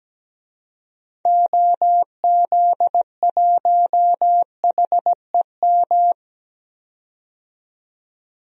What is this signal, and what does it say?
Morse code, heard as a keyed tone, about 17 wpm, 700 Hz. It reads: OZ1HEM
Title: DXCC-100 Call Signs - 6 Characters - 17wpm